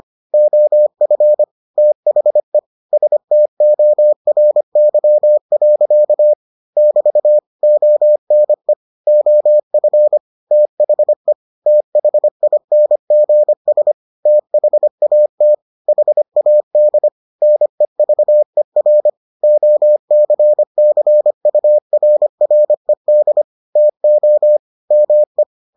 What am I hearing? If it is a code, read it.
Morse code, heard as a keyed tone, about 25 wpm, 600 Hz. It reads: OF THE STORY. <BT> ONE OF THE THINGS THAT HAD NEVER OCCURRED TO ME